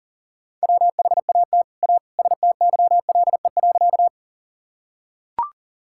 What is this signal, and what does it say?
Morse code, heard as a keyed tone, about 37 wpm, 700 Hz. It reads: WHAT A STYLE.
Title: Sentences from Top 1000 Words - Review Part 5 - 40wpm